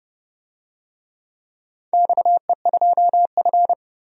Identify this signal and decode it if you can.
Morse code, heard as a keyed tone, about 30 wpm, 700 Hz. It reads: XE2F
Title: DXCC-100 Call Signs - 4 Characters - 30wpm